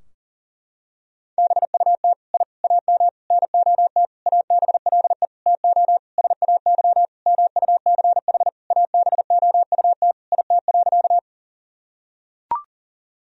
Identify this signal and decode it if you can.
Morse code, heard as a keyed tone, about 37 wpm, 700 Hz. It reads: BUT I AM NOT ABLE TO SAY MUCH ABOUT IT.
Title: Sentences from Top 1000 Words - Review Part 2 - 40wpm